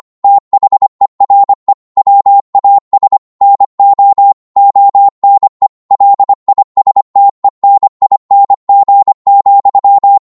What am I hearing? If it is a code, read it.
Morse code, heard as a keyed tone, about 25 wpm, 800 Hz. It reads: THERE WAS NO ONE LISTENING,